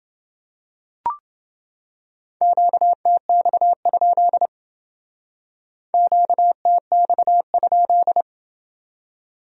Morse code, heard as a keyed tone, about 29 wpm, 700 Hz. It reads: QTX? QTX?